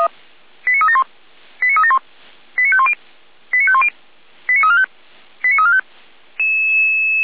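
Five-tone selective call (ZVEI), heard as quick runs of five tones, tones of about 70 ms. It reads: 89371 89371 89510 89510 89345 89345
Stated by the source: ZVEI five-tone selective call 89305 received on SDR, from the Feuerwehr Orlamünde audio archive.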